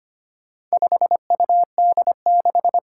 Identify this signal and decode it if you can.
Morse code, heard as a keyed tone, about 25 wpm, 700 Hz. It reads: 5UD6
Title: Four Challenging Letter-Number Combinations 25wpm